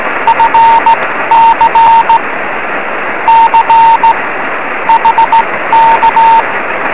Morse code, heard as a keyed tone, about 17 wpm, spaced as for 14 wpm, 910 Hz, with radio noise. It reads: FC C HK